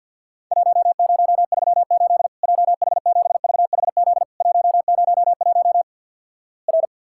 Morse code, heard as a keyed tone, about 52 wpm, 700 Hz. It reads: 1038 JH7VHZ 101 R
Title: Worked All Europe DX Contest - QTC Exchange - 50wpm